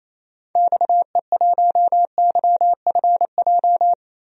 Morse code, heard as a keyed tone, about 28 wpm, 700 Hz. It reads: XE1YFJ
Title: DXCC-100 Call Signs - 6 Characters - 28wpm